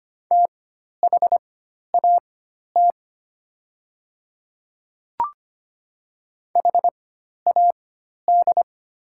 Morse code, heard as a keyed tone, about 25 wpm, spaced as for 12 wpm, 700 Hz. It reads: THAT HAD
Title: 4x Character Spacing - Top 100 Words - 25wpm